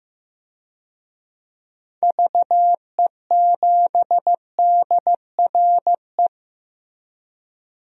Morse code, heard as a keyed tone, about 16 wpm, 700 Hz. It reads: VE7DRE
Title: DXCC-100 Call Signs - 6 Characters - Repeated 2x - 15wpm